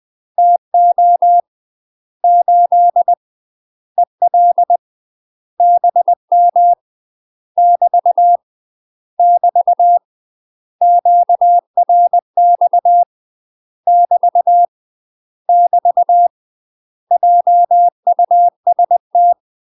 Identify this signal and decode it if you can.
Morse code, heard as a keyed tone, about 20 wpm, 700 Hz. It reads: TO 8 EL BM <BT> <BT> QRX <BT> <BT> JUST
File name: UK MASTER QSOs TRAINING / 20WPM / UK MASTER QSOs 0002